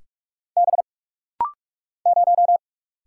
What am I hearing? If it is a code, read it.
Morse code, heard as a keyed tone, about 43 wpm, 700 Hz. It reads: B 0
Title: Single Letter-Number - Rapid-Fire 45wpm